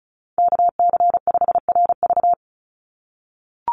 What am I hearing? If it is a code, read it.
Morse code, heard as a keyed tone, about 34 wpm, 700 Hz. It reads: KC5RV
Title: USA Call Signs 35wpm